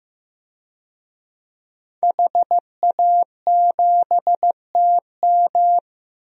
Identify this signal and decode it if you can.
Morse code, heard as a keyed tone, about 16 wpm, 700 Hz. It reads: HA7TM
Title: DXCC-100 Call Signs - 5 Characters - Repeated 2x - 15wpm